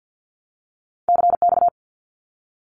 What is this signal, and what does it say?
Morse code, heard as a keyed tone, about 50 wpm, 700 Hz. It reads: CX
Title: Abbreviations Repeated 3x - 50wpm